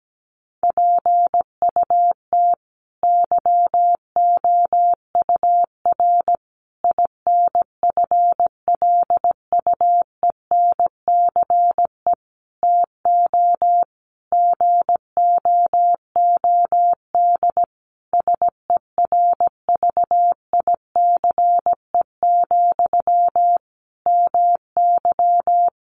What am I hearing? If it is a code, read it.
Morse code, heard as a keyed tone, about 17 wpm, 700 Hz. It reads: PUT YOUR INFLUENCE TO GOOD SERVICE, MY
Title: Sentences from Top 1100 Words 17wpm